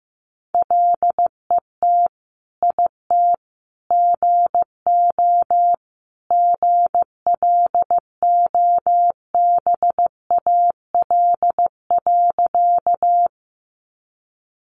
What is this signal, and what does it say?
Morse code, heard as a keyed tone, about 15 wpm, 700 Hz. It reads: LET IT GO GLOBAL.
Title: Sentences from Top 1200 Words 15wpm